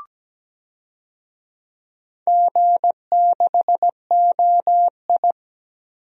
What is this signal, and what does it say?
Morse code, heard as a keyed tone, about 17 wpm, 700 Hz. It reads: G6OI
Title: DXCC-100 Call Signs - 4 Characters - 17wpm